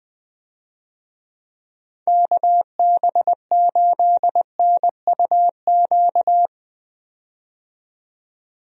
Morse code, heard as a keyed tone, about 20 wpm, 700 Hz. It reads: KB8NUQ
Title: USA Call Signs - 6 Characters - 20wpm